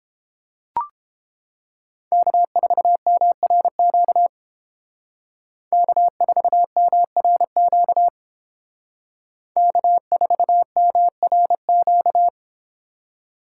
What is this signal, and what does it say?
Morse code, heard as a keyed tone, about 31 wpm, 700 Hz. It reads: K4MRQ K4MRQ K4MRQ